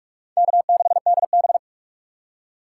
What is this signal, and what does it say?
Morse code, heard as a keyed tone, about 47 wpm, 700 Hz. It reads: K6DB
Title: USA Call Signs - 4 Characters - 6x Character Spacing - 45wpm